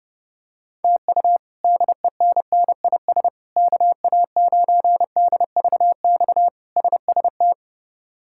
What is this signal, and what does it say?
Morse code, heard as a keyed tone, about 32 wpm, 700 Hz. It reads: TU DENNIS KA9DVX SST
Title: K1USN SST QSO Practice - 30wpm